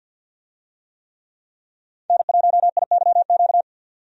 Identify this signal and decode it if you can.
Morse code, heard as a keyed tone, about 52 wpm, 700 Hz. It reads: N1IYQ
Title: USA Call Signs - 5 Characters - 50wpm